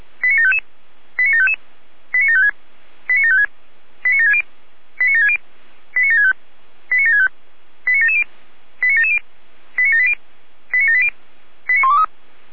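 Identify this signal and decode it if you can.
Five-tone selective call (ZVEI), heard as quick runs of five tones, tones of about 70 ms. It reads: 89755 89755 89756 89756 89760 89760 89765 89765 89770 89770 89780 89780 89123
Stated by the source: ZVEI five-tone call received by SDR.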